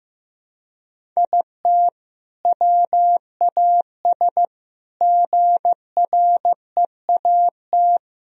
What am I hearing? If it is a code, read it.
Morse code, heard as a keyed tone, about 15 wpm, 700 Hz. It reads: IT WAS GREAT
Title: Sentences from Top 1000 Words - Review Part 4 - 15wpm